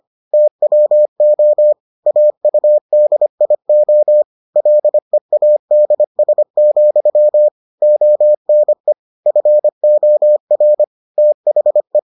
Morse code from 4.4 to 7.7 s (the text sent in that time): LEADS,